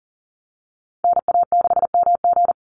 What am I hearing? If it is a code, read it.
Morse code, heard as a keyed tone, about 40 wpm, 700 Hz. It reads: NA6MG
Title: NA SOTA Activator Callsigns 40wpm